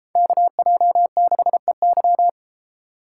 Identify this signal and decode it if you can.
Morse code, heard as a keyed tone, about 33 wpm, 700 Hz. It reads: KJ6EY